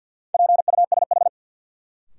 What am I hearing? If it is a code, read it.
Morse code, heard as a keyed tone, about 47 wpm, 700 Hz. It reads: PUSH
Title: Sentences from Top 1300 Words - New Word Drill Alt - 50wpm